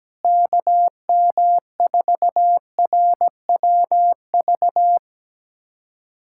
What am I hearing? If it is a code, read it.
Morse code, heard as a keyed tone, about 17 wpm, 700 Hz. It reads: KM4RWV
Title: USA Callsign DE Callsign 17wpm